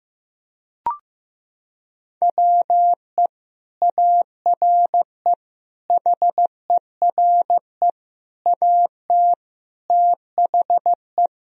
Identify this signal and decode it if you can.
Morse code, heard as a keyed tone, about 15 wpm, 700 Hz. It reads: WE ARE HERE AT THE
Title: Sentences from Top 1000 Words - Review Part 1 - 15wpm